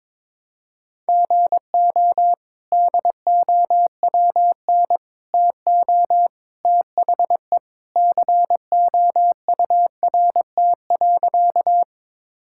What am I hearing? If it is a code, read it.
Morse code, heard as a keyed tone, about 22 wpm, 700 Hz. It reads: GO DOWN TO THE COURT.
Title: Sentences from Top 1600 Words 22wpm